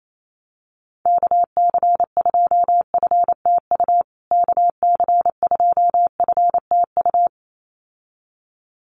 Morse code, heard as a keyed tone, about 28 wpm, 700 Hz. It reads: KC2FTU KC2FTU
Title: USA Call Signs Repeated 2x 28wpm